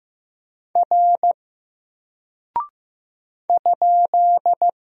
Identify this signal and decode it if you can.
Morse code, heard as a keyed tone, about 15 wpm, 700 Hz. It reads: R ?
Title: Single Character (Letter:Number:Punctuation:Prosign) 15wpm